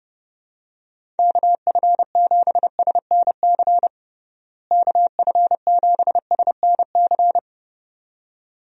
Morse code, heard as a keyed tone, about 32 wpm, 700 Hz. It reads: KF7SNC KF7SNC
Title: USA Call Signs - 6 Characters - Repeated 2x - 30wpm